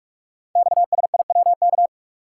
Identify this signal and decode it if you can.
Morse code, heard as a keyed tone, about 44 wpm, 700 Hz. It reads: XSIWK
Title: Code Groups - 5 Letters - 45wpm